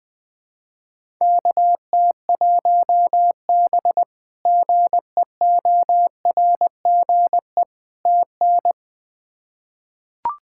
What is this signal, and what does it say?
Morse code, heard as a keyed tone, about 20 wpm, 700 Hz. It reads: KT1B GEORGE TN
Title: K1USN SST QSO Practice - 20wpm